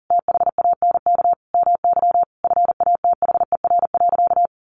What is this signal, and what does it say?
Morse code, heard as a keyed tone, about 40 wpm, 700 Hz. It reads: THANK MY FATHER.
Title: Sentences from Top 1100 Words 40wpm